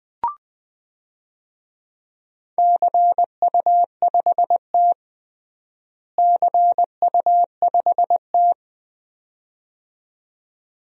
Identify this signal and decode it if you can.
Morse code, heard as a keyed tone, about 21 wpm, 700 Hz. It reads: CU5T CU5T
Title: DXCC-100 Call Signs - 4 Characters - Repeated 2x - 20wpm